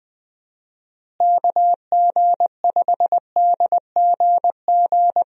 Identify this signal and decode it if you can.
Morse code, heard as a keyed tone, about 20 wpm, 700 Hz. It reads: KG5DGG
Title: USA Call Signs - 6 Characters - Repeated 3x - 20wpm